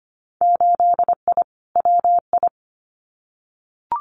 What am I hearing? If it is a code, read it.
Morse code, heard as a keyed tone, about 25 wpm, 690 Hz. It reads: 8I WI
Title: Winter Field Day Exchanges 25wpm